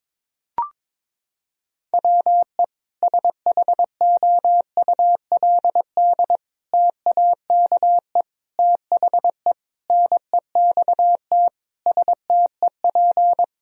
Morse code, heard as a keyed tone, about 22 wpm, 700 Hz. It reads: WE SHOULD TAKE THE NEXT STEP
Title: Sentences from Top 1000 Words - Review Part 2 - 22wpm